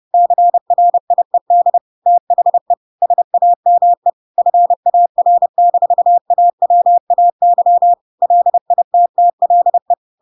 Morse code, heard as a keyed tone, about 30 wpm, 700 Hz. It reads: CRIED THE SAME FAR-AWAY LITTLE
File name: Pinocchio / Pinocchio ch1 30 30 ik1hge